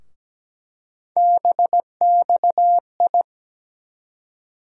Morse code, heard as a keyed tone, about 17 wpm, 700 Hz. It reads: BXI
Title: USA Call Sign Postfixes - 3 Letters 17wpm